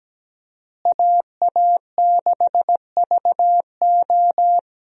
Morse code, heard as a keyed tone, about 17 wpm, 700 Hz. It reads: AA6VO
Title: USA Call Signs - 5 Characters - Repeated 2x - 17wpm